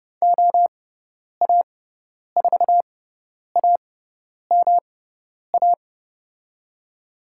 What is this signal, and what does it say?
Morse code, heard as a keyed tone, about 30 wpm, spaced as for 10 wpm, 700 Hz. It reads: OA4AMA
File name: Lesson 18g - Letter m - Call Signs 30wpm